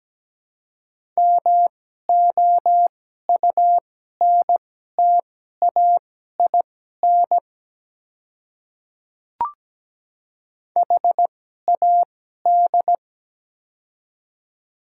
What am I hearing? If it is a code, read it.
Morse code, heard as a keyed tone, about 17 wpm, spaced as for 12 wpm, 700 Hz. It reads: MOUNTAIN HAD
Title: 2x Character Spacing - Top 300 Words - 17wpm